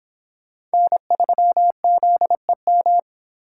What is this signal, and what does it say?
Morse code, heard as a keyed tone, about 25 wpm, 700 Hz. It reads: N3ZEM